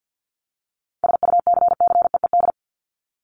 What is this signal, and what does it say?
Morse code, heard as a keyed tone, about 50 wpm, 700 Hz. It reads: SUCCEED
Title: Top 1501-1750 Words 50wpm